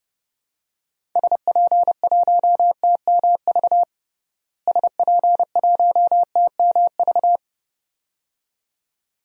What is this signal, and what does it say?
Morse code, heard as a keyed tone, about 30 wpm, 700 Hz. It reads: SP1TMV SP1TMV
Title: DXCC-100 Call Signs - 6 Characters - Repeated 2x - 30wpm